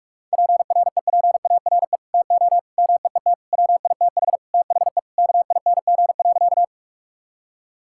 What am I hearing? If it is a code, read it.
Morse code, heard as a keyed tone, about 46 wpm, 700 Hz. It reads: PREPARE TO MEET WITH THE KING.
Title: Sentences from Top 1000 Words - Review Part 6 - 45wpm